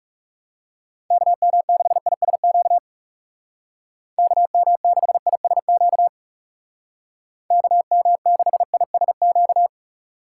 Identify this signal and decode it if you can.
Morse code, heard as a keyed tone, about 40 wpm, 700 Hz. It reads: KM6ISQ KM6ISQ KM6ISQ